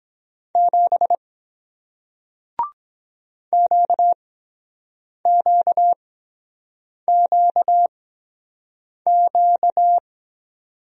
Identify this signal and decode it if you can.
Morse code, heard as a keyed tone, about 23 wpm, spaced as for 7 wpm, 700 Hz. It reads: 7 QQQQ